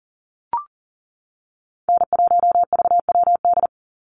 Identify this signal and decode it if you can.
Morse code, heard as a keyed tone, about 40 wpm, 700 Hz. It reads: N1VWD
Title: NA SOTA Chaser Callsigns 40wpm